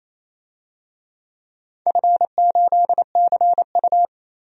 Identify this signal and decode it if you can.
Morse code, heard as a keyed tone, about 28 wpm, 700 Hz. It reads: F8CU
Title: DXCC-100 Call Signs - 4 Characters - 28wpm